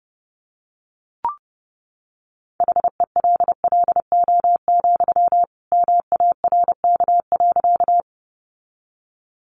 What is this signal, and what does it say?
Morse code, heard as a keyed tone, about 30 wpm, 700 Hz. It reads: HELLO, MARK.
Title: Sentences from Top 1100 Words 30wpm